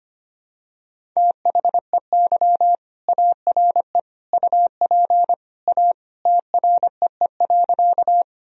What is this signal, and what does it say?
Morse code, heard as a keyed tone, about 25 wpm, 700 Hz. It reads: THEY ARE UP A TREE.
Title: Sentences from Top 1000 Words - Review Part 1 - 25wpm